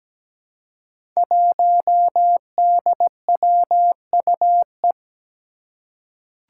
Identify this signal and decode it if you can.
Morse code, heard as a keyed tone, about 17 wpm, 700 Hz. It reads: 1DWUE
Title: Five Challenging Letter-Number Combinations 17wpm